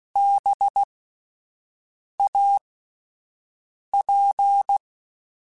Morse code, heard as a keyed tone, about 16 wpm, spaced as for 5 wpm, 800 Hz. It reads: BAP
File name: Alphabetical / Alphabetical mp3s / Bs @5